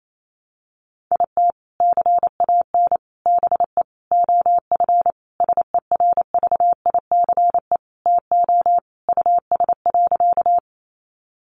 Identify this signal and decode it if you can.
Morse code, heard as a keyed tone, about 28 wpm, 700 Hz. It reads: IT CAN BE OF SERVICE TO US.
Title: Sentences from Top 1200 Words 28wpm